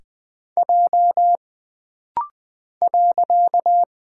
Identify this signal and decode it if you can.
Morse code, heard as a keyed tone, about 20 wpm, 700 Hz. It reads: J .
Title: Single Character (Letter:Number:Punctuation:Prosign) - Rapid-Fire - 20wpm